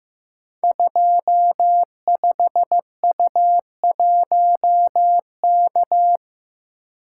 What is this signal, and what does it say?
Morse code, heard as a keyed tone, about 15 wpm, 700 Hz. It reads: 25U1K
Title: Code Groups - 5 Letter:Numbers - 15wpm